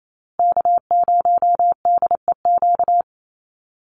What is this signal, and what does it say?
Morse code, heard as a keyed tone, about 28 wpm, 700 Hz. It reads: K0DEQ
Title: Call Signs Repeated 2x - 28wpm